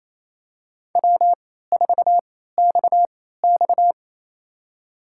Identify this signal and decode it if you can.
Morse code, heard as a keyed tone, about 28 wpm, spaced as for 16 wpm, 700 Hz. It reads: W4XX
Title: USA Call Signs - 4 Characters - 3x Character Spacing - 28wpm